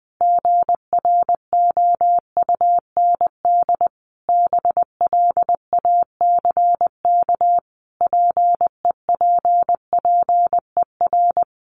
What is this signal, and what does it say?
Morse code, heard as a keyed tone, about 20 wpm, 700 Hz. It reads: GROUND BLACK PEPPER
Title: Sets of 5 Words Encore 20wpm